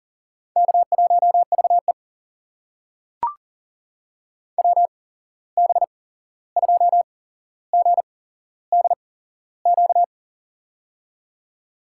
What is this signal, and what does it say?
Morse code, heard as a keyed tone, about 39 wpm, spaced as for 15 wpm, 700 Hz. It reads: K1VE WB2GDQ
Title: USA Call Signs - 8x Character Spacing - 40wpm